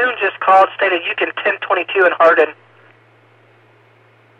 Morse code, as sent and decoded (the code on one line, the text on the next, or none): none